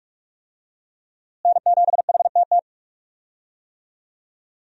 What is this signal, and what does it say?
Morse code, heard as a keyed tone, about 42 wpm, 700 Hz. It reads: N7HTT
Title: USA Call Signs - 5 Characters - Repeated 2x - 45wpm